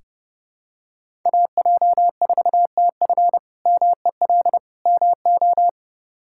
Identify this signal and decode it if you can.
Morse code, heard as a keyed tone, about 30 wpm, 700 Hz. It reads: AJ4TF MEL MO
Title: K1USN SST QSO Practice - 30wpm